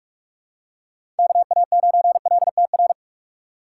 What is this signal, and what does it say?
Morse code, heard as a keyed tone, about 46 wpm, 700 Hz. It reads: KA9LTR
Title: USA Call Signs - 6 Characters - 6x Character Spacing - 45wpm